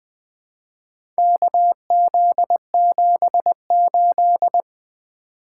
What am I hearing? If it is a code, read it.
Morse code, heard as a keyed tone, about 20 wpm, 700 Hz. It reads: KZ78
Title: Four Letter-Number Combinations - 20wpm